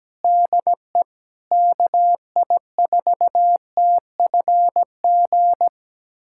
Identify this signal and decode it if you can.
Morse code, heard as a keyed tone, about 17 wpm, 700 Hz. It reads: DE KI4TFG
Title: USA Callsign DE Callsign - 6 Characters - 17wpm